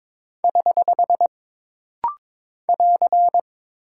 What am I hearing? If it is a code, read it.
Morse code, heard as a keyed tone, about 22 wpm, 700 Hz. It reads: <HH> <AR>
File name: practice / 22wpm / Single Punctuation-Prosign - Rapid-Fire - 22wpm